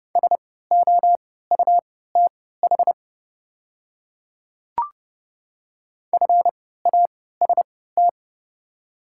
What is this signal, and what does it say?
Morse code, heard as a keyed tone, about 31 wpm, spaced as for 17 wpm, 690 Hz. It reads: SOUTH FAST
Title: 3x Character Spacing - Top 500 Words - 30wpm